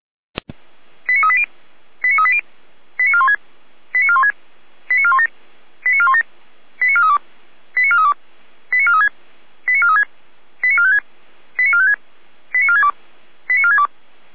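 Five-tone selective call (ZVEI), heard as quick runs of five tones, tones of about 70 ms. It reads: 89380 89380 89416 89416 89417 89417 89432 89432 89436 89436 89456 89456 89462 89462
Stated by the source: ZVEI five-tone selective call received on SDR.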